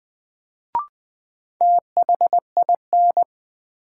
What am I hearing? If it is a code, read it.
Morse code, heard as a keyed tone, about 20 wpm, 700 Hz. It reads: THIN
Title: Sentences from Top 1200 Words - New Word Drill - 20wpm